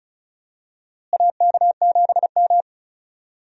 Morse code, heard as a keyed tone, about 36 wpm, 700 Hz. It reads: AK7M
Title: USA Call Signs - 4 Characters - Repeated 2x - 35wpm